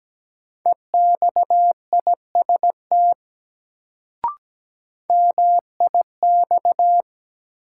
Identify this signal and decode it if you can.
Morse code, heard as a keyed tone, about 17 wpm, 700 Hz. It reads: EXIST MIX
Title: Sentences from Top 1100 Words - New Word Drill Alt - 17wpm